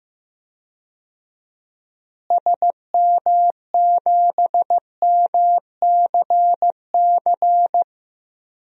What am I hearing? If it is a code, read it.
Morse code, heard as a keyed tone, about 15 wpm, 700 Hz. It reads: SM7MCC
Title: DXCC-100 Call Signs - 6 Characters - Repeated 2x - 15wpm